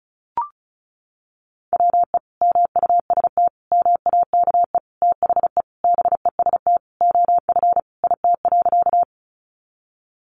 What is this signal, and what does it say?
Morse code, heard as a keyed tone, about 34 wpm, 700 Hz. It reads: WE MUST MAKE THE BEST OF IT.
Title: Sentences from Top 400 Words 35wpm